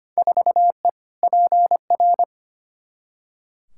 Morse code, heard as a keyed tone, about 25 wpm, 700 Hz. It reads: 4E PR
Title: Field Day Exchanges Encore - 25wpm